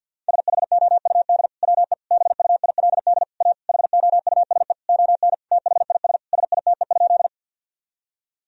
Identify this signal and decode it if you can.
Morse code, heard as a keyed tone, about 50 wpm, 700 Hz. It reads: SHOUD WE BUILD A HOUSE ON THIS SITE?
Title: Sentences from Top 1000 Words - Review Part 4 - 50wpm